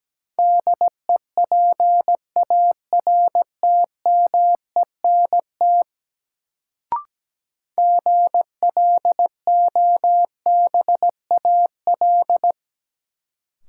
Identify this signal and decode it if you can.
Morse code, heard as a keyed tone, about 17 wpm, 700 Hz. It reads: DEPARTMENT GLOBAL
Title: Sentences from Top 1200 Words - New Word Drill Alt - 17wpm